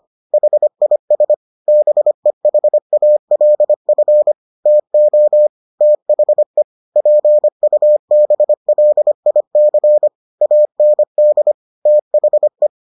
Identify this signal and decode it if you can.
Morse code, heard as a keyed tone, about 25 wpm, 600 Hz. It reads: HIS BEHALF TO THE PUBLIC AND THE